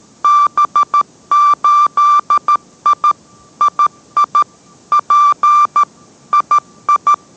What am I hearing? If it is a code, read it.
Morse code, heard as a keyed tone, about 13 wpm, 1180 Hz. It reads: B8I II P II